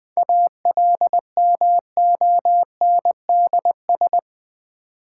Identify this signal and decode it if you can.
Morse code, heard as a keyed tone, about 20 wpm, 700 Hz. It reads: ALMONDS
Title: Difficult Words with Hints - 20wpm